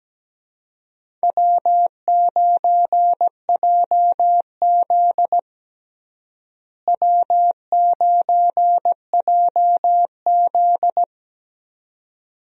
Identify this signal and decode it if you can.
Morse code, heard as a keyed tone, about 17 wpm, 700 Hz. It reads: W9JZ W9JZ
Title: USA Call Signs - 4 Characters - Repeated 2x - 17wpm